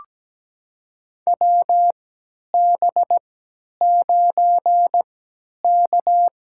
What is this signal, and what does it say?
Morse code, heard as a keyed tone, about 17 wpm, spaced as for 10 wpm, 700 Hz. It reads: WB9K
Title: USA Call Signs - 3x Character Spacing - 17wpm